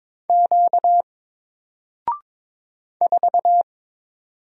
Morse code, heard as a keyed tone, about 22 wpm, 700 Hz. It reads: Q 4
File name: Single-Letter-Number-22wpm